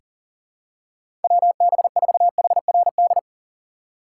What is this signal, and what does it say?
Morse code, heard as a keyed tone, about 40 wpm, 700 Hz. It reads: WB4HRD
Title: USA Call Signs - 6 Characters - 6x Character Spacing - 40wpm